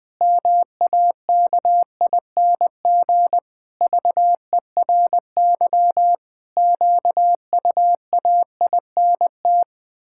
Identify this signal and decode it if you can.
Morse code, heard as a keyed tone, about 20 wpm, 700 Hz. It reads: MAKING VERY QUAINT
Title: Pangram Challenge v2 - 20wpm